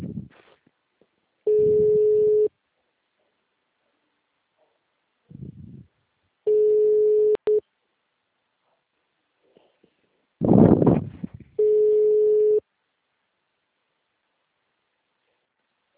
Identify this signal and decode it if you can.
Ringback tone (Europe, 425 Hz).